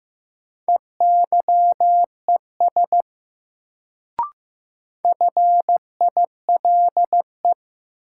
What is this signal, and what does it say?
Morse code, heard as a keyed tone, about 15 wpm, 700 Hz. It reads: EYES FILE
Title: Sentences from Top 1000 Words - Review Part 4 - New Word Drill - 15wpm